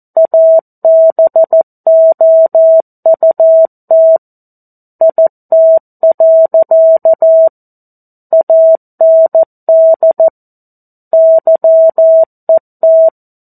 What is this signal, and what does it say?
Morse code, heard as a keyed tone, about 14 wpm, 650 Hz. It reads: ABOUT IT. AND YET
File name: 02-Chapter-02 14 WPM